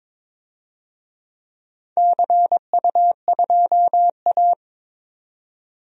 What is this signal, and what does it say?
Morse code, heard as a keyed tone, about 21 wpm, 700 Hz. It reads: CU2A
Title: DXCC-100 Call Signs - 4 Characters - 22wpm